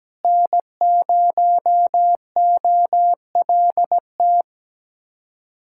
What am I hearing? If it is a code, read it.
Morse code, heard as a keyed tone, about 17 wpm, 700 Hz. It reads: N0OLT
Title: USA Callsign DE Callsign - 5 Characters - 17wpm